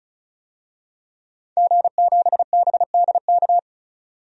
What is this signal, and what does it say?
Morse code, heard as a keyed tone, about 36 wpm, 700 Hz. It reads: G7BDK
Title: DXCC-100 Call Signs - 5 Characters - 35wpm